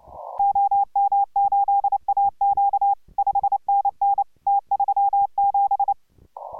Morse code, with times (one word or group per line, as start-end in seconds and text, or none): 0.4-2.9 OM8AQ
3.2-4.2 5NN
4.5-5.9 T37